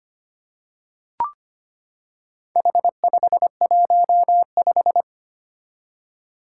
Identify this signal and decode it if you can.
Morse code, heard as a keyed tone, about 25 wpm, 700 Hz. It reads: H515
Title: Four Challenging Letter-Number Combinations 25wpm